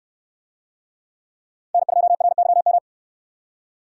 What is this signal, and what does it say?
Morse code, heard as a keyed tone, about 65 wpm, 700 Hz. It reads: N2RYM